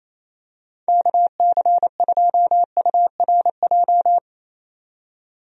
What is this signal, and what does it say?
Morse code, heard as a keyed tone, about 27 wpm, 700 Hz. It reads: KC2URJ
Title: USA Call Signs - 6 Characters - 28wpm